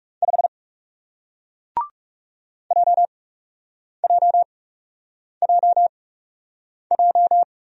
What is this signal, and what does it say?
Morse code, heard as a keyed tone, about 38 wpm, spaced as for 8 wpm, 700 Hz. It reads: 5 JJJJ